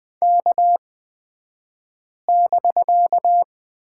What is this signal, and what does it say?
Morse code, heard as a keyed tone, about 20 wpm, spaced as for 5 wpm, 700 Hz. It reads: K<BK>
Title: 254 Lesson 41d - Period - Two Characters 20wpm